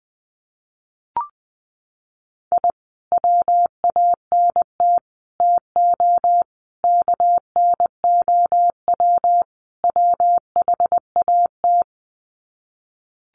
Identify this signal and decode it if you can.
Morse code, heard as a keyed tone, about 19 wpm, 700 Hz. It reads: I WANT TO KNOW WHAT
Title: Sets of 5 Words Encore2 20wpm